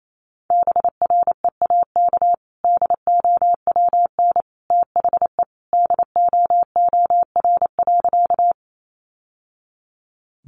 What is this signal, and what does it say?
Morse code, heard as a keyed tone, about 28 wpm, 690 Hz. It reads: BREAK DOWN THE DOOR.
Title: Sentences from Top 900 Words 28wpm